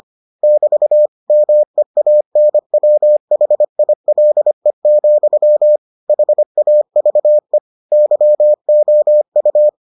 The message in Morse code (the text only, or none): <BT> MEANWHILE, HAVE YOU